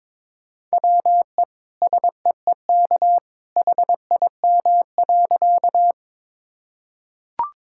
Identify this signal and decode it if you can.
Morse code, heard as a keyed tone, about 23 wpm, 700 Hz. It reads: WE SEEK HIM.
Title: Sentences from Top 1600 Words 22wpm